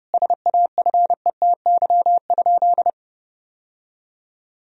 Morse code, heard as a keyed tone, about 32 wpm, 700 Hz. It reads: SAFETY?
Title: Sentences from Top 1000 Words - Review Part 4 - 30wpm